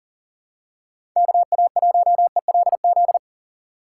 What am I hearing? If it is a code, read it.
Morse code, heard as a keyed tone, about 40 wpm, 700 Hz. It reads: KA1ELZ
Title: USA Call Signs - 6 Characters - 2x Character Spacing - 40wpm